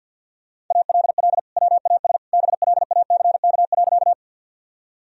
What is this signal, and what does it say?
Morse code, heard as a keyed tone, about 49 wpm, 700 Hz. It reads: ALL WAS BLACK.
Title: Sentences from Top 1000 Words - Review Part 2 - 50wpm